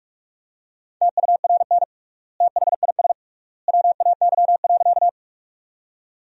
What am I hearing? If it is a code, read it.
Morse code, heard as a keyed tone, about 41 wpm, 700 Hz. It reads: TURN THIS WAY.
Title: 3x Word-Spacing - Sentences from Top 200 Words - 45wpm